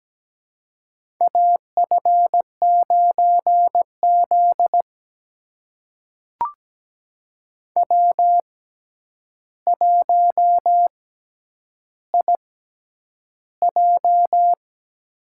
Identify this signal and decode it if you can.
Morse code, heard as a keyed tone, about 17 wpm, spaced as for 9 wpm, 700 Hz. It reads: AF9Z W1IJ
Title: USA Call Signs - 4 Characters - 6x Character Spacing - 17wpm